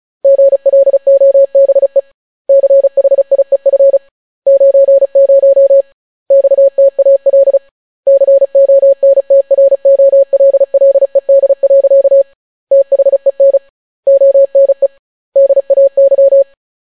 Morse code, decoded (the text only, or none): GLOBE CHIEF 90 XTAL CONTROLLED. THEN ONE DAY